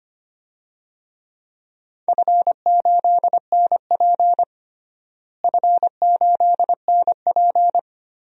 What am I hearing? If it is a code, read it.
Morse code, heard as a keyed tone, about 25 wpm, 700 Hz. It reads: F8NP F8NP
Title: DXCC-100 Call Signs - 4 Characters - Repeated 3x - 25wpm